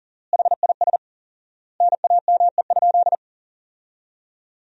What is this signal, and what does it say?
Morse code, heard as a keyed tone, about 37 wpm, 700 Hz. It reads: HIS NAME?
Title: 4x Word-Spacing - Sentences from Top 400 Words - 40wpm